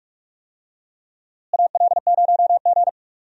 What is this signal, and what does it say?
Morse code, heard as a keyed tone, about 46 wpm, 700 Hz. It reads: AL0G
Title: USA Call Signs - 4 Characters - Repeated 2x - 45wpm